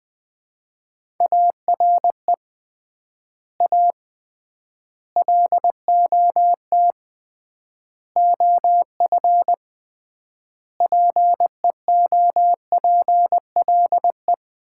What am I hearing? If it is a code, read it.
Morse code, heard as a keyed tone, about 20 wpm, 700 Hz. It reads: ARE A LOT OF PEOPLE
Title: 3x Word-Spacing - Sets of 5 Words - 20wpm